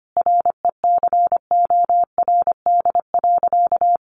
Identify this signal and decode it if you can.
Morse code, heard as a keyed tone, about 25 wpm, 700 Hz. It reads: RECORD.
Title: Sentences from Top 1400 Words 25wpm